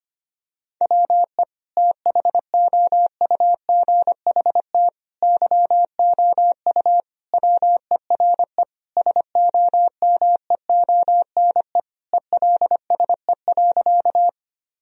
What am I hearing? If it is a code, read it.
Morse code, heard as a keyed tone, about 25 wpm, 700 Hz. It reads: WE THOUGHT YOU WERE SOMEONE ELSE.
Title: Sentences from Top 1000 Words - Review Part 4 - 25wpm